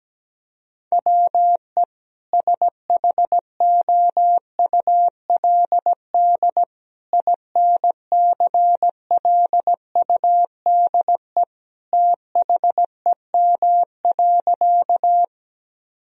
Morse code, heard as a keyed tone, about 17 wpm, 700 Hz. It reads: WE SHOULD INCLUDE THEM.
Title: Sentences from Top 1000 Words - Review Part 3 - 17wpm